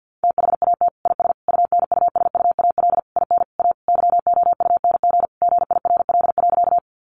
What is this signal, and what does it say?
Morse code, heard as a keyed tone, about 50 wpm, 700 Hz. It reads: THAT IS UNUSUAL IN A YOUNG GIRL.
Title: Sentences from Top 1400 Words 50wpm